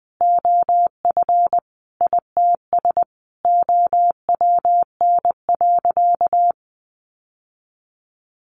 Morse code, heard as a keyed tone, about 20 wpm, 700 Hz. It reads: OF ITS OWN.
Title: Sentences from Top 1200 Words 20wpm